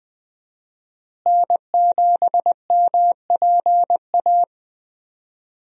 Morse code, heard as a keyed tone, about 20 wpm, 700 Hz. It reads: N7MPA
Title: USA Call Signs - 5 Characters - Repeated 3x - 20wpm